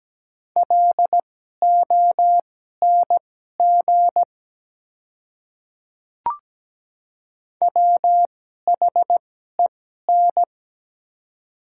Morse code, heard as a keyed tone, about 17 wpm, spaced as for 12 wpm, 700 Hz. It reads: LONG WHEN
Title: 2x Character Spacing - Top 100 Words - 17wpm